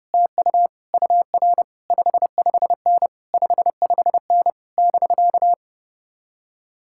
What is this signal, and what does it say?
Morse code, heard as a keyed tone, about 33 wpm, 700 Hz. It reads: TU UR 55N 55N <BK>